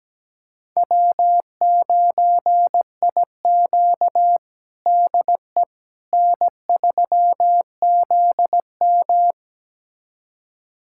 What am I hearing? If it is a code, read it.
Morse code, heard as a keyed tone, about 17 wpm, 700 Hz. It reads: W9IQ DE N3ZM
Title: USA Callsign DE Callsign - 4 Characters - 17wpm